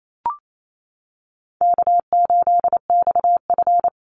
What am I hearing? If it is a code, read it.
Morse code, heard as a keyed tone, about 28 wpm, 700 Hz. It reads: K8XF
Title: Easy Call Signs Repeated 2x - 28wpm